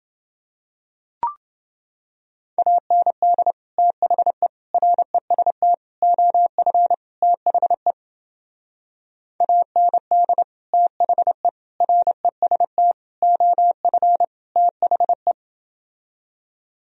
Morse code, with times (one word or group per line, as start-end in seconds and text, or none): 2.6-3.5 AND
3.8-4.5 THE
4.7-5.7 REST
6.0-6.9 OF
7.2-7.9 THE
9.4-10.4 AND
10.7-11.5 THE
11.8-12.9 REST
13.2-14.2 OF
14.6-15.3 THE